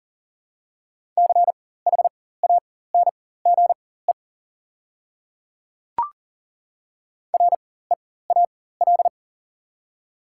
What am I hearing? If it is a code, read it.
Morse code, heard as a keyed tone, about 40 wpm, spaced as for 19 wpm, 690 Hz. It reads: CHANGE REAL
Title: 4x Character Spacing - Top 500 Words - 40wpm